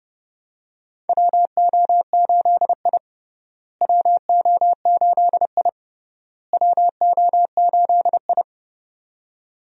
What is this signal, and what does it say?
Morse code, heard as a keyed tone, about 32 wpm, 700 Hz. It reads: WO8I WO8I WO8I
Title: USA Call Signs - 4 Characters - Repeated 3x - 30wpm